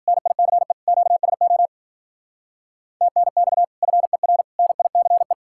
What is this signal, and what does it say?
Morse code, heard as a keyed tone, about 47 wpm, 700 Hz. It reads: NICE QSO TNX FER NICE
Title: Speed-Racing Common QSO Phrases Encore - 40wpm